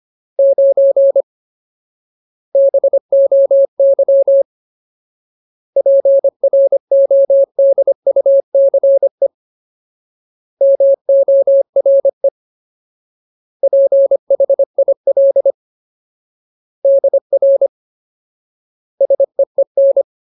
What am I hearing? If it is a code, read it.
Morse code, heard as a keyed tone, about 25 wpm, 550 Hz. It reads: 9 BOY PRODUCE MORE PHIL DR SEEN
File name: Ultimate Word List / 25WPM Ultimate